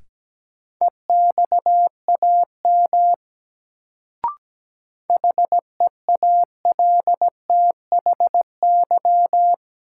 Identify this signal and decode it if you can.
Morse code, heard as a keyed tone, about 17 wpm, 700 Hz. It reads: EXAM HEALTHY
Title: Sentences from Top 1000 Words - Review Part 5 - New Word Drill - 17wpm